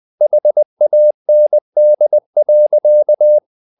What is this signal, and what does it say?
Morse code, heard as a keyed tone, about 20 wpm, 600 Hz. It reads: HAND.
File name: the old man and the sea part 2 20wpm